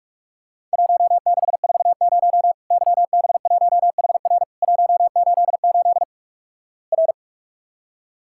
Morse code, heard as a keyed tone, about 48 wpm, 700 Hz. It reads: 1640 YB1HR 188 R
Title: Worked All Europe DX Contest - QTC Exchange - 45wpm